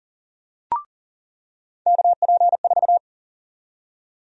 Morse code, heard as a keyed tone, about 38 wpm, 700 Hz. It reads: KP4
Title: USA Call Sign Prefixes - 40wpm